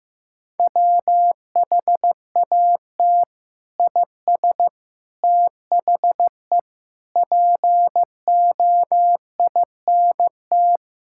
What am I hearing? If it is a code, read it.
Morse code, heard as a keyed tone, about 15 wpm, 700 Hz. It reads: WHAT IS THE POINT
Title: Sentences from Top 1000 Words - Review Part 5 - 15wpm